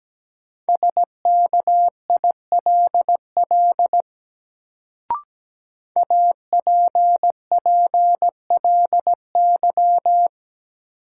Morse code, heard as a keyed tone, about 17 wpm, 700 Hz. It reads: SKILL APPLY
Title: Sentences from Top 1200 Words - New Word Drill - 17wpm